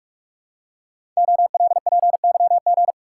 Morse code, heard as a keyed tone, about 44 wpm, 700 Hz. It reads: OLPYG
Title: Code Groups - 5 Letters - 45wpm